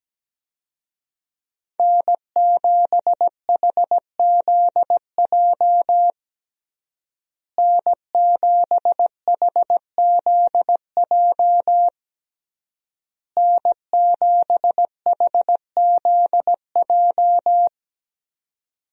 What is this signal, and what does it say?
Morse code, heard as a keyed tone, about 17 wpm, 700 Hz. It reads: N7HZJ N7HZJ N7HZJ
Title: USA Call Signs - 5 Characters - Repeated 3x - 17wpm